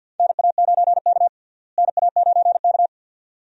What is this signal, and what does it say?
Morse code, heard as a keyed tone, about 47 wpm, 700 Hz. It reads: NA9K NA9K
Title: USA Call Signs - 4 Characters - Repeated 2x - 50wpm